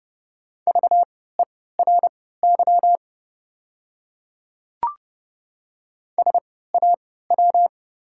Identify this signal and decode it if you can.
Morse code, heard as a keyed tone, about 30 wpm, spaced as for 17 wpm, 700 Hz. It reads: VERY SAW
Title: 3x Character Spacing - Top 500 Words - 30wpm